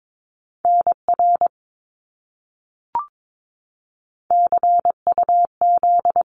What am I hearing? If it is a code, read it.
Morse code, heard as a keyed tone, about 22 wpm, 700 Hz. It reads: NR CUZ
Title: Abbreviations Repeated 3x - 22wpm